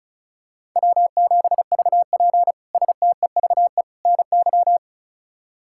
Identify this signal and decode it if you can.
Morse code, heard as a keyed tone, about 36 wpm, 700 Hz. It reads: W7VP STEVE NY
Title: K1USN SST QSO Practice - 35wpm